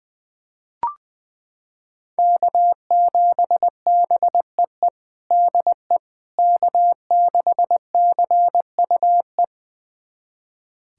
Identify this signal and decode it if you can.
Morse code, heard as a keyed tone, about 20 wpm, 700 Hz. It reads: K7BEE DE K6CUE
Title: USA Callsign DE Callsign - 5 Characters - 20wpm